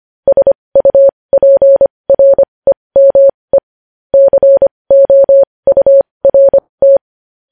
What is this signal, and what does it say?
Morse code, heard as a keyed tone, about 25 wpm, spaced as for 20 wpm, 550 Hz. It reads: SUPREME COURT
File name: podcast / News headlines-1x-25x20